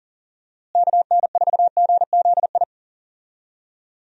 Morse code, heard as a keyed tone, about 40 wpm, 700 Hz. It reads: KN4GZI
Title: USA Call Signs - 6 Characters - Repeated 3x - 40wpm